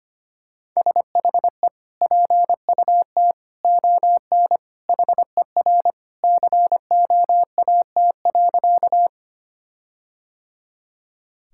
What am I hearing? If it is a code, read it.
Morse code, heard as a keyed tone, about 25 wpm, 700 Hz. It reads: SHE PUT ON HER COAT.
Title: Sentences from Top 1600 Words 25wpm